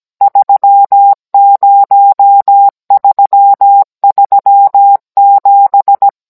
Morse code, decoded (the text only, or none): 30337